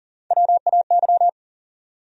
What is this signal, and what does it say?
Morse code, heard as a keyed tone, about 40 wpm, 700 Hz. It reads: WAY
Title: Top 100 Words Repeated 40wpm